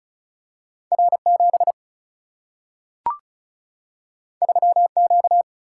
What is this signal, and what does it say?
Morse code, heard as a keyed tone, about 36 wpm, 700 Hz. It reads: R7 3Q
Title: Single-Letter Single-Number 35wpm